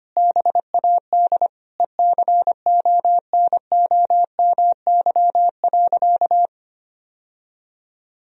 Morse code, heard as a keyed tone, about 25 wpm, 700 Hz. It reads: BAD ECONOMY.
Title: Sentences from Top 1000 Words - Review Part 3 - 25wpm